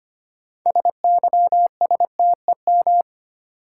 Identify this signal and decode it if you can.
Morse code, heard as a keyed tone, about 25 wpm, 700 Hz. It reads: SYSTEM
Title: Sentences from Top 1000 Words - Review Part 3 - New Word Drill - 25wpm